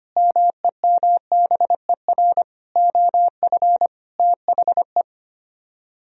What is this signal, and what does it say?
Morse code, heard as a keyed tone, about 31 wpm, 700 Hz. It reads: MEMBER OF THE